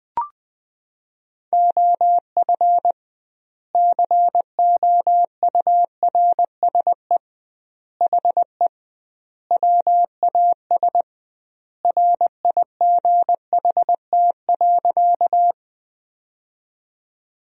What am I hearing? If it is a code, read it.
Morse code, heard as a keyed tone, about 20 wpm, 700 Hz. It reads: OF COURSE HE WAS RIGHT.
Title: 2x Word-Spacing - Sentences from Top 500 Words - 20wpm